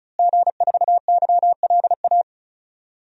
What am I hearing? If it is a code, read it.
Morse code, heard as a keyed tone, about 36 wpm, 700 Hz. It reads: G4YLA
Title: DXCC-100 Call Signs - 5 Characters - Repeated 3x - 35wpm